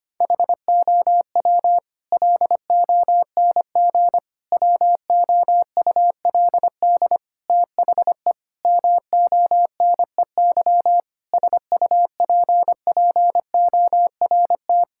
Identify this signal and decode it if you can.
Morse code, heard as a keyed tone, about 25 wpm, 700 Hz. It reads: HOW LONG WOULD THE MONEY SUPPORT
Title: Sentences from Top 1000 Words - Review Part 5 - 25wpm